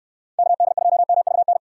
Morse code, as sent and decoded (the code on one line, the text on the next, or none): -.- -.. ..--- -- ...- -.
KD2MVN